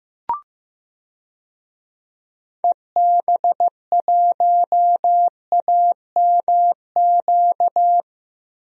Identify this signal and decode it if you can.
Morse code, heard as a keyed tone, about 15 wpm, 700 Hz. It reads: EB1AMQ
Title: DXCC-100 Call Signs - 6 Characters - 15wpm